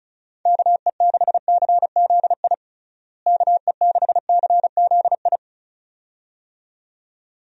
Morse code, heard as a keyed tone, about 36 wpm, 700 Hz. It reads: KE6CZI KE6CZI
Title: USA Call Signs - 6 Characters - Repeated 2x - 35wpm